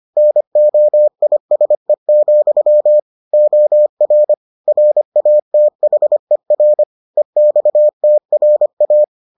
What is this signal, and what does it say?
Morse code, heard as a keyed tone, about 25 wpm, 600 Hz. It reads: NOISE, OR RATHER EXTRA